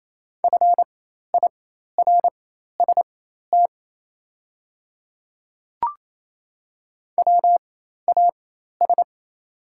Morse code, heard as a keyed tone, about 28 wpm, spaced as for 13 wpm, 700 Hz. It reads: FIRST WAS
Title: 4x Character Spacing - Top 100 Words - 28wpm